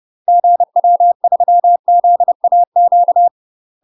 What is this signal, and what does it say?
Morse code, heard as a keyed tone, about 30 wpm, 700 Hz. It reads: GW3ZAQ
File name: UK MASTER QSOs TRAINING / 30WPM / UK MASTER QSOs 0019